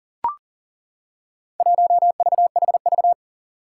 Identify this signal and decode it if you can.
Morse code, heard as a keyed tone, about 40 wpm, 700 Hz. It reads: 1VHV
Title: Four Challenging Letter-Number Combinations 40wpm